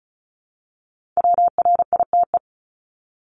Morse code, heard as a keyed tone, about 36 wpm, 690 Hz. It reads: WRITE
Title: Multi-letter Phonemes 35wpm